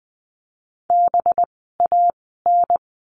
Morse code, heard as a keyed tone, about 19 wpm, 700 Hz. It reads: B A N
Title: 185 Lesson 30f - Letter b - Words 20wpm